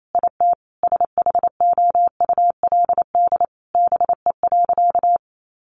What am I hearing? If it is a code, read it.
Morse code, heard as a keyed tone, about 28 wpm, 700 Hz. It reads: IT SHOULD BE.
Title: Sentences from Top 900 Words 28wpm